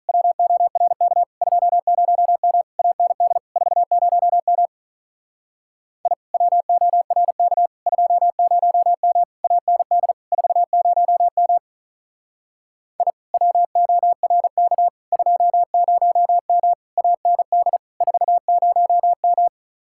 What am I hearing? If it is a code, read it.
Morse code, heard as a keyed tone, about 44 wpm, 700 Hz. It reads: WORK 20M AND 40M I WORK 20M AND 40M I WORK 20M AND 40M
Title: Speed-Racing Common QSO Phrases - 35wpm